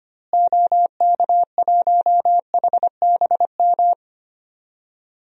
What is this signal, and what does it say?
Morse code, heard as a keyed tone, about 25 wpm, 700 Hz. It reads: OK1HBM
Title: DXCC-100 Call Signs - 6 Characters - Repeated 2x - 25wpm